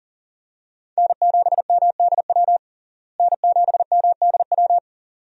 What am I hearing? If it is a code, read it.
Morse code, heard as a keyed tone, about 40 wpm, 700 Hz. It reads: N7MDW N7MDW
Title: USA Call Signs - 5 Characters - Repeated 2x - 40wpm